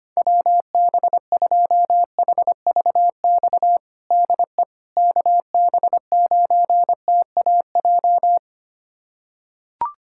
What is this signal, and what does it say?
Morse code, heard as a keyed tone, about 25 wpm, 700 Hz. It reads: WB2HVX DE KB9TAJ
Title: USA Callsign DE Callsign - 6 Characters - 25wpm